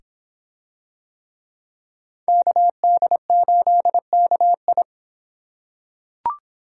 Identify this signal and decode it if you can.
Morse code, heard as a keyed tone, about 26 wpm, 700 Hz. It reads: KD8KI